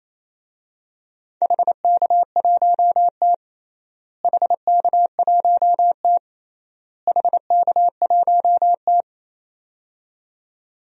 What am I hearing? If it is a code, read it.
Morse code, heard as a keyed tone, about 28 wpm, 700 Hz. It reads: HK1T HK1T HK1T
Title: DXCC-100 Call Signs - 4 Characters - Repeated 3x - 28wpm